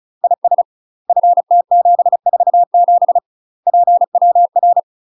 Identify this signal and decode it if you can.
Morse code, heard as a keyed tone, about 34 wpm, 700 Hz. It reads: IS FT747 PWR
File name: UK MASTER QSOs TRAINING / 35WPM / UK MASTER QSOs 0035